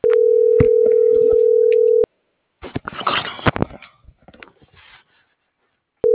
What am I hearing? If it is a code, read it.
Ringback tone (North America), 440 + 480 Hz.